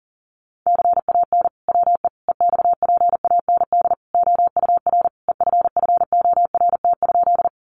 Morse code, heard as a keyed tone, about 40 wpm, 700 Hz. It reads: CAN WE EXPAND OUR EFFORT?
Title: Sentences from Top 1400 Words 40wpm